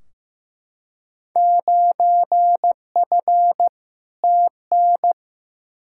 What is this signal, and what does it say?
Morse code, heard as a keyed tone, about 15 wpm, 700 Hz. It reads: 9F TN
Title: Field Day Exchanges Encore - 15wpm